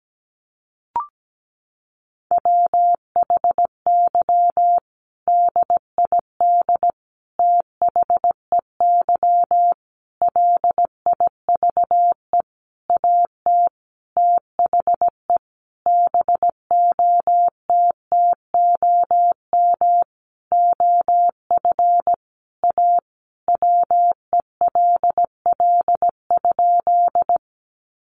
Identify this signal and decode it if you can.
Morse code, heard as a keyed tone, about 17 wpm, 700 Hz. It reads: WHY DID THEY LIVE AT THE BOTTOM OF A WELL?
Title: Sentences from Top 900 Words 17wpm